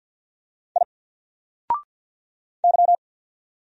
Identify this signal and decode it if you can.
Morse code, heard as a keyed tone, about 48 wpm, 700 Hz. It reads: I Y
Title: Single Character (Letter:Number:Punctuation:Prosign) 50wpm